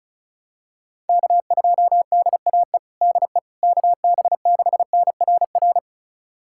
Morse code, heard as a keyed tone, about 34 wpm, 700 Hz. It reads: K2DAE DE KB6NRR
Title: USA Callsign DE Callsign 35wpm